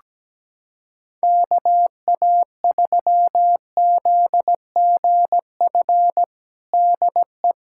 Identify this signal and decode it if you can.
Morse code, heard as a keyed tone, about 17 wpm, 700 Hz. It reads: KA3ZGF DE
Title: USA Callsign DE Callsign 17wpm